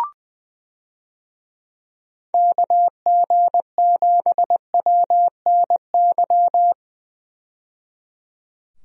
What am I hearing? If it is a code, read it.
Morse code, heard as a keyed tone, about 20 wpm, 700 Hz. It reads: KG7WNY
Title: USA Call Signs - 6 Characters - 20wpm